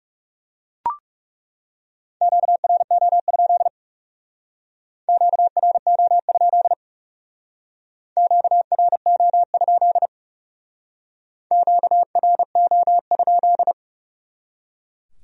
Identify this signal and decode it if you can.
Morse code, heard as a keyed tone, about 38 wpm, 700 Hz. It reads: QRO? QRO? QRO? QRO?